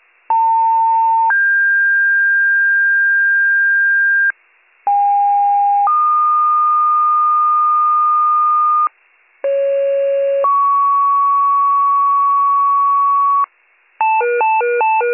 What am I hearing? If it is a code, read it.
Two-tone page, heard as a one-second tone, then a three-second tone, about 900 then 1600 Hz, about 810 then 1220 Hz, about 550 then 1120 Hz. It is heard from 0.3 to 13.5 s.